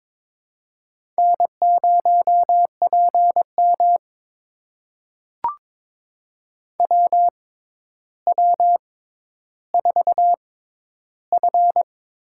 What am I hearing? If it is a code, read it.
Morse code, heard as a keyed tone, about 21 wpm, spaced as for 11 wpm, 700 Hz. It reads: N0PM WW4F
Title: USA Call Signs - 4 Characters - 6x Character Spacing - 22wpm